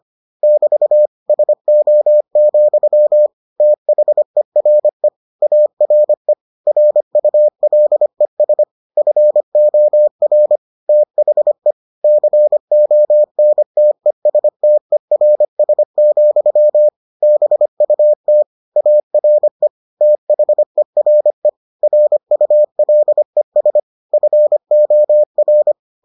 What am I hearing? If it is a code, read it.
Morse code, heard as a keyed tone, about 25 wpm, 600 Hz. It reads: <BT> SO, THERE ARE RULES FOR THE CONTESTERS, BUT ARE THERE RULES FOR